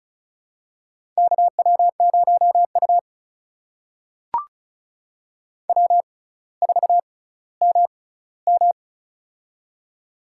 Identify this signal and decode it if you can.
Morse code, heard as a keyed tone, about 36 wpm, spaced as for 18 wpm, 700 Hz. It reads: KW0U W4MM
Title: USA Call Signs - 4 Characters - 6x Character Spacing - 35wpm